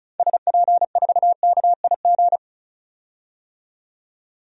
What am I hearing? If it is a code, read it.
Morse code, heard as a keyed tone, about 36 wpm, 700 Hz. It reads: SP4KIG
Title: DXCC-100 Call Signs - 6 Characters - 35wpm